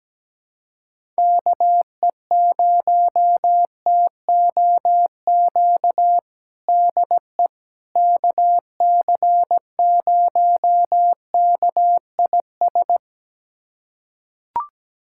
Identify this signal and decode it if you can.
Morse code, heard as a keyed tone, about 17 wpm, 700 Hz. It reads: KE0TOQ DE KC0KIS
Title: USA Callsign DE Callsign - 6 Characters - 17wpm